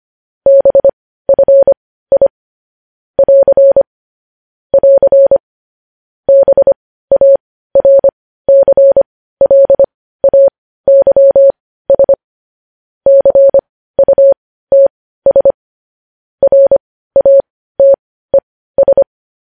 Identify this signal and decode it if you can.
Morse code, heard as a keyed tone, about 25 wpm, spaced as for 15 wpm, 550 Hz. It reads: BFI <AR> <AR> BARCLAYS CUTS RATES
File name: podcast / News headlines-1x-25x15